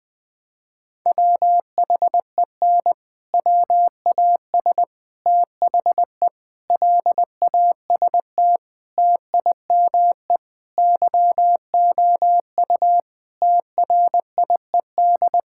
Morse code, heard as a keyed tone, about 20 wpm, 700 Hz. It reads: WHEN WAS THE LAST TIME YOU TRIED
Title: Sentences with 6-Word Phrases - 20wpm